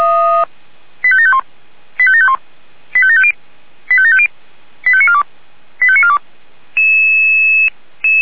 Five-tone selective call (ZVEI), heard as quick runs of five tones, tones of about 70 ms. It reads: 85751 85751 85750 85750 85942 85942
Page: SDR/ZVEI